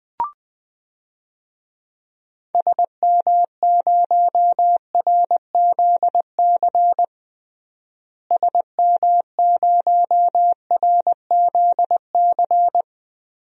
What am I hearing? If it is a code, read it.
Morse code, heard as a keyed tone, about 20 wpm, 700 Hz. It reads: SM0RZC SM0RZC
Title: DXCC-100 Call Signs - 6 Characters - Repeated 2x - 20wpm